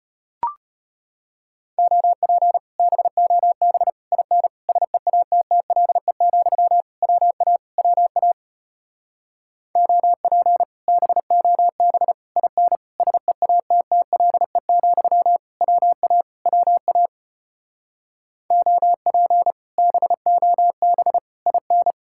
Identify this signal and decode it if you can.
Morse code, heard as a keyed tone, about 33 wpm, 700 Hz. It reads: OP BOB IN SEATTLE, WA WA OP BOB IN SEATTLE, WA WA OP BOB IN